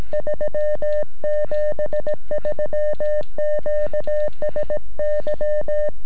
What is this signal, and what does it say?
Morse code, heard as a keyed tone, about 17 wpm, 600 Hz. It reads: 3 73 QSY
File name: sm7lcb / 2121 sm7gep 9cm